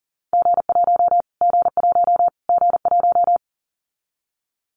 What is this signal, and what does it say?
Morse code, heard as a keyed tone, about 37 wpm, 700 Hz. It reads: G1 G1 G1
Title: Callsign Prefixes for Top 25 DXCC - Repeated 3x - 40wpm